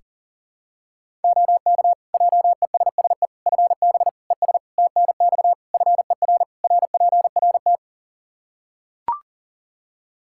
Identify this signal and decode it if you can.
Morse code, heard as a keyed tone, about 40 wpm, 700 Hz. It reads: OK JESSE FB ES TNX FER RPRT
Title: Practice QSO - Exchange 2 - Info - 40wpm